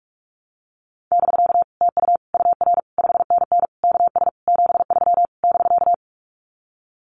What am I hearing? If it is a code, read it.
Morse code, heard as a keyed tone, about 44 wpm, 700 Hz. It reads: <BK> TU UR 5NN KS 73 <BK>
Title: Common QSO Phrases Encore 45wpm